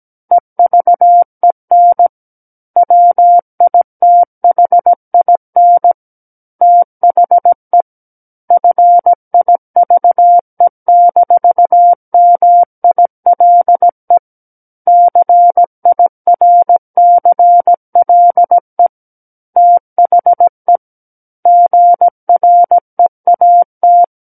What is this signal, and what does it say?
Morse code, heard as a keyed tone, about 17 wpm, 700 Hz. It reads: EVEN WITHIN THE FIVE-MILE CIRCLE THE GREAT
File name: Book One / 08-Chapter 8 - 17 WPM